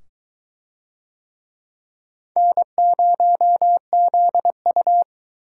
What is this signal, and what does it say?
Morse code, heard as a keyed tone, about 23 wpm, 700 Hz. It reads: N0ZU